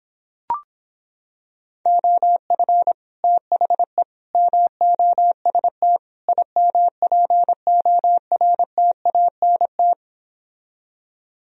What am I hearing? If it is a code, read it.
Morse code, heard as a keyed tone, about 26 wpm, 700 Hz. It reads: OF THE MOST IMPORTANT